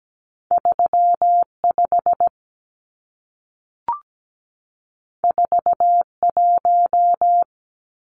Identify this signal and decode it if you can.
Morse code, heard as a keyed tone, about 17 wpm, 700 Hz. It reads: 35 41
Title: Numbers - 2-Digit 17wpm